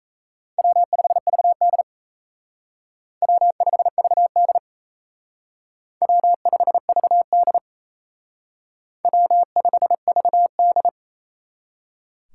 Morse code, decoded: W5VD W5VD W5VD W5VD

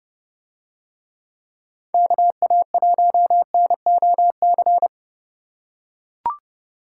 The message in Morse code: KA1NOC

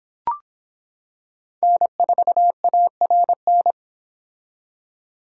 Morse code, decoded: N4ARN

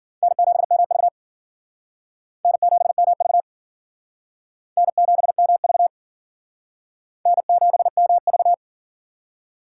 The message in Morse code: N7MV N7MV N7MV N7MV

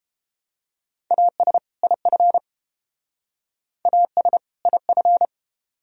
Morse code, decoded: AS IF AS IF

